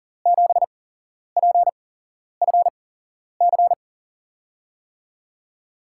Morse code, heard as a keyed tone, about 40 wpm, spaced as for 11 wpm, 700 Hz. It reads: 7PFC
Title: USA Call Signs - 6 Characters - 8x Character Spacing - 40wpm